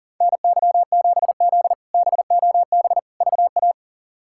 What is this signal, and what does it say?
Morse code, heard as a keyed tone, about 40 wpm, 700 Hz. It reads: NY7Z BOB VA
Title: K1USN SST QSO Practice - 40wpm